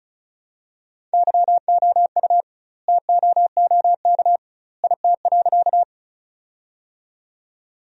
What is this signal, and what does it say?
Morse code, heard as a keyed tone, about 36 wpm, 700 Hz. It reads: YOU TOOK IT.
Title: 2x Word-Spacing - Sentences from Top 300 Words - 35wpm